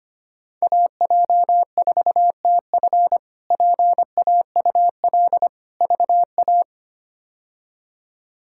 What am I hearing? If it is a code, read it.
Morse code, heard as a keyed tone, about 25 wpm, 700 Hz. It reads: AJ4TF PAUL VA
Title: K1USN SST QSO Practice - 25wpm